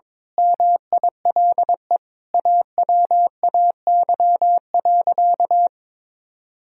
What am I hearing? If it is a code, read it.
Morse code, heard as a keyed tone, about 22 wpm, 700 Hz. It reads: MILE AWAY.
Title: Sentences from Top 1000 Words - Review Part 2 - 22wpm